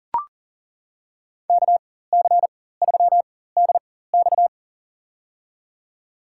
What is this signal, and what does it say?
Morse code, heard as a keyed tone, about 40 wpm, spaced as for 19 wpm, 700 Hz. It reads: KC3DX
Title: USA Call Signs - 4x Character Spacing - 40wpm